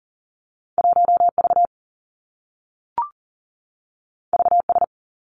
Morse code, heard as a keyed tone, about 40 wpm, 700 Hz. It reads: 1 V V S
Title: Two Challenging Letter-Number Combinations 40wpm